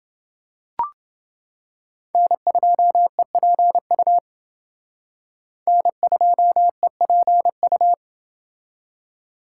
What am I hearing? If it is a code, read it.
Morse code, heard as a keyed tone, about 28 wpm, 700 Hz. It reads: N2EPU N2EPU